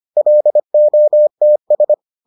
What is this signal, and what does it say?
Morse code, heard as a keyed tone, about 25 wpm, 600 Hz. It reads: LOTS